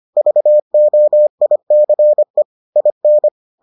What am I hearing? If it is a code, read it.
Morse code, heard as a keyed tone, about 25 wpm, 600 Hz. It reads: VOICE IN